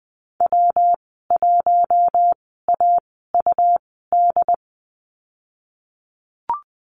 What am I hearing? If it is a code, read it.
Morse code, heard as a keyed tone, about 20 wpm, 700 Hz. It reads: W 1 A U D
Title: 154 Lesson 25g - Letter p - Call Signs 20wpm